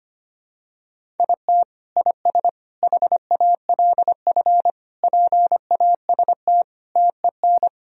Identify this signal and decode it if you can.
Morse code, heard as a keyed tone, about 25 wpm, 700 Hz. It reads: IT IS HALF PAST TEN
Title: Sentences from Top 1000 Words - Review Part 3 - 25wpm